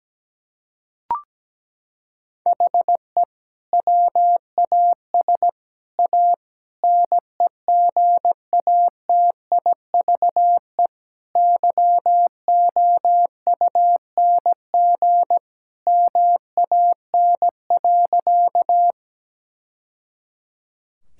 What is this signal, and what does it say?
Morse code, heard as a keyed tone, about 17 wpm, 700 Hz. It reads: HE WAS A NEGATIVE YOUNG MAN.
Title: Sentences from Top 1000 Words - Review Part 6 - 17wpm